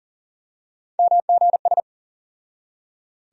Morse code, heard as a keyed tone, about 40 wpm, 700 Hz. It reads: MGS
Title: USA Call Sign Postfixes - 3 Letters 40wpm